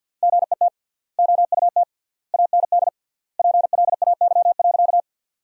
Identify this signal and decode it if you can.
Morse code, heard as a keyed tone, about 51 wpm, 700 Hz. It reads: GET OUT AND PLAY.
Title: 3x Word-Spacing - Sentences from Top 200 Words - 50wpm